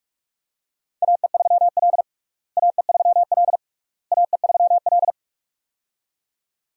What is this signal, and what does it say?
Morse code, heard as a keyed tone, about 47 wpm, 700 Hz. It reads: AE3L AE3L AE3L
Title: USA Call Signs - 4 Characters - Repeated 3x - 45wpm